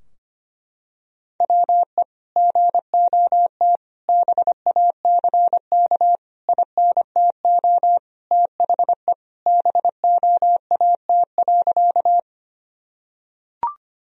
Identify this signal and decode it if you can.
Morse code, heard as a keyed tone, about 25 wpm, 700 Hz. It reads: WE GOT BACK INTO THE BOAT.
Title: Sentences from Top 1000 Words - Review Part 3 - 25wpm